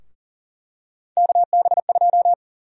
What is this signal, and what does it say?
Morse code, heard as a keyed tone, about 40 wpm, 700 Hz. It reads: KB2
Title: USA Call Sign Prefixes - Repeated 2x - 40wpm